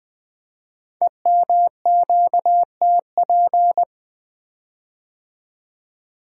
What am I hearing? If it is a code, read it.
Morse code, heard as a keyed tone, about 20 wpm, 700 Hz. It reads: EMQTP
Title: Code Groups - 5 Letters - 20wpm